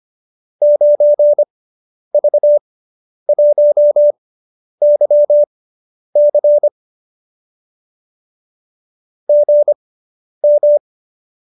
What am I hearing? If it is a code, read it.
Morse code, heard as a keyed tone, about 25 wpm, spaced as for 10 wpm, 590 Hz. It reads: 9V1YC GM